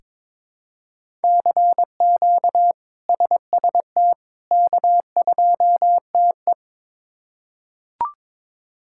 Morse code, heard as a keyed tone, about 21 wpm, 700 Hz. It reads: CQ SST K2TE
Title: K1USN SST QSO Practice - 22wpm